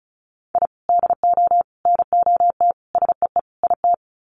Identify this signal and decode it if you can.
Morse code, heard as a keyed tone, about 34 wpm, 700 Hz. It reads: I DO NOT SEE IT
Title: Sets of 5 Words Encore2 35wpm